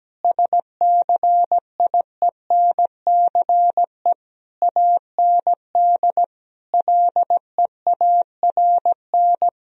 Morse code, heard as a keyed tone, about 17 wpm, 700 Hz. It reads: SCIENCE AND LEARN
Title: Sentences from Top 1000 Words - Review Part 4 - 17wpm